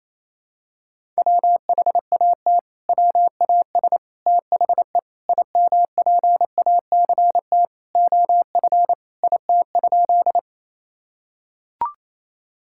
Morse code, heard as a keyed tone, about 28 wpm, 700 Hz. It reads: WHAT WAS THE IMPACT OF IT?
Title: Sentences from Top 1000 Words - Review Part 6 - 28wpm